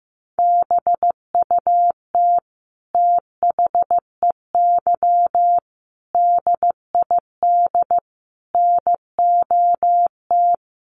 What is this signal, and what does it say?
Morse code, heard as a keyed tone, about 15 wpm, 700 Hz. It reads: BUT THEY DID NOT
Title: Sentences from Top 1100 Words 15wpm